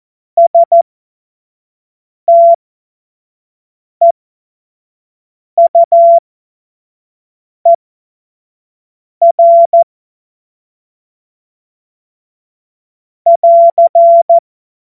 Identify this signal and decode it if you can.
Morse code, heard as a keyed tone, about 14 wpm, spaced as for 5 wpm, 680 Hz. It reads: STEUER <AR>
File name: Morsekurs / Witze / Witz 20. Juni 2023 25 F70